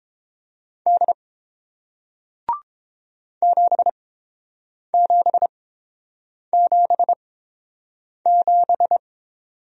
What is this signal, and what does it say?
Morse code, heard as a keyed tone, about 30 wpm, spaced as for 7 wpm, 700 Hz. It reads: D 7777